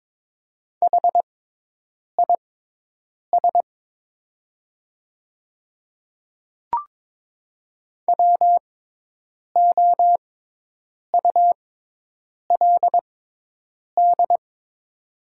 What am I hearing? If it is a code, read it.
Morse code, heard as a keyed tone, about 22 wpm, spaced as for 8 wpm, 700 Hz. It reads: HIS WOULD
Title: 6x Character Spacing - Top 300 Words - 22wpm